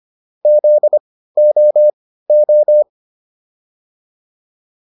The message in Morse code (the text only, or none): ZOO